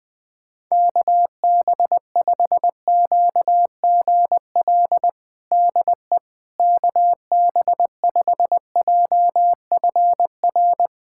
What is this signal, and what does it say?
Morse code, heard as a keyed tone, about 21 wpm, 700 Hz. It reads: KB5QGL DE KB5JFR
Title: USA Callsign DE Callsign - 6 Characters - 20wpm